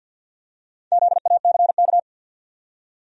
Morse code, heard as a keyed tone, about 49 wpm, 700 Hz. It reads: ZACK
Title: USA Common Names - Max 5 Letters - 50wpm